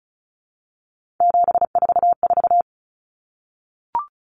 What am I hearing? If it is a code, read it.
Morse code, heard as a keyed tone, about 34 wpm, 700 Hz. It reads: 744
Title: Common Cut Numbers - 3-Digit 35wpm